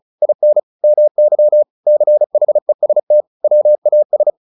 Morse code, heard as a keyed tone, about 34 wpm, 600 Hz. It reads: IN MY CHEST WAS
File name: the old man and the sea 2 35wpm